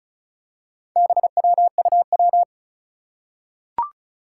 Morse code, heard as a keyed tone, about 36 wpm, 700 Hz. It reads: BWUW
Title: Four Challenging Letter-Number Combinations 35wpm